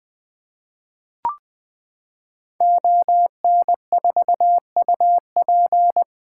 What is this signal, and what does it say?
Morse code, heard as a keyed tone, about 20 wpm, 700 Hz. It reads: ON4UP
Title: 2023 SOTA Activator Call Signs - 20wpm